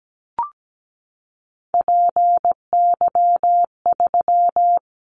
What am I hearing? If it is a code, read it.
Morse code, heard as a keyed tone, about 17 wpm, 700 Hz. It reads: PY3
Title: Callsign Prefixes for Top 100 DXCC - 17wpm